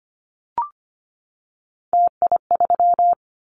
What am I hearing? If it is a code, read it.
Morse code, heard as a keyed tone, about 25 wpm, 700 Hz. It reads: TI3
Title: Callsign Prefixes for Top 100 DXCC - 25wpm